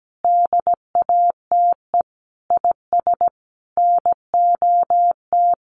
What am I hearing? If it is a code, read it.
Morse code, heard as a keyed tone, about 17 wpm, 700 Hz. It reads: DATE IS NOT
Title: Sentences from Top 1300 Words 17wpm